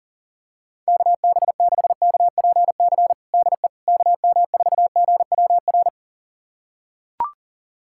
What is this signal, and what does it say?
Morse code, heard as a keyed tone, about 37 wpm, 700 Hz. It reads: KB6KPC DE KM4GWR
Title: USA Callsign DE Callsign - 6 Characters - 40wpm